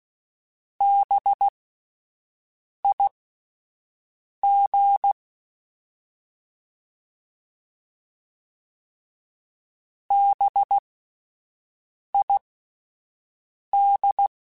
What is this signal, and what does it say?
Morse code, heard as a keyed tone, about 16 wpm, spaced as for 5 wpm, 800 Hz. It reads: BIG BID
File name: Spaced Alphas / Spaced Alphas mp3s / Bs sp @10